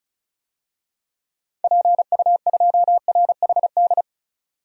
Morse code, heard as a keyed tone, about 36 wpm, 700 Hz. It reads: PU2RHD
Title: DXCC-100 Call Signs - 6 Characters - 35wpm